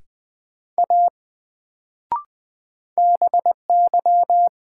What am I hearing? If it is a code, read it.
Morse code, heard as a keyed tone, about 20 wpm, 700 Hz. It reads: A BY
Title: Sentences from Top 100 Words - New Word Drill - 20wpm